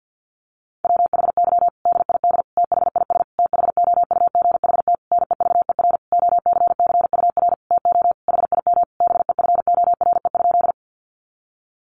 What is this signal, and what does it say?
Morse code, heard as a keyed tone, about 50 wpm, 700 Hz. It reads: WHY DID THIS THOUGHT NEVER OCCUR TO HIM BEFORE?
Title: Sentences from Top 900 Words 50wpm